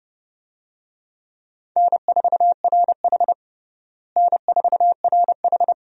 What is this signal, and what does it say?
Morse code, heard as a keyed tone, about 30 wpm, 700 Hz. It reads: N4RH N4RH
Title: USA Call Signs - 4 Characters - Repeated 2x - 30wpm